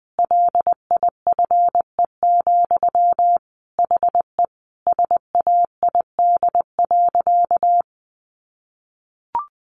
Morse code, heard as a keyed tone, about 20 wpm, 700 Hz. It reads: LIFE, HE SAID.
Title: Sentences from Top 900 Words 20wpm